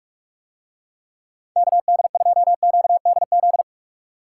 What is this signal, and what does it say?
Morse code, heard as a keyed tone, about 44 wpm, 700 Hz. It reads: KD2QDZ